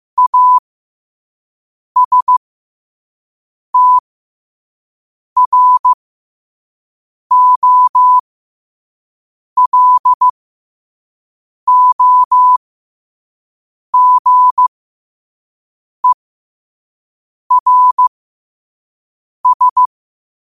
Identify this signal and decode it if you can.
Morse code, heard as a keyed tone, about 15 wpm, spaced as for 5 wpm, 1000 Hz. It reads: ASTROLOGERS
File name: qotdfiles / 5wpm